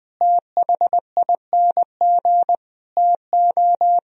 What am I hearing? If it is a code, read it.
Morse code, heard as a keyed tone, about 21 wpm, 700 Hz. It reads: THING TO
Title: Speed-Racing - Sets of 5 Words 20wpm